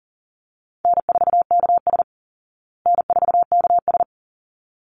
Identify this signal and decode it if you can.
Morse code, heard as a keyed tone, about 40 wpm, 700 Hz. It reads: N4KS N4KS
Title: Easy Call Signs Repeated 2x - 40wpm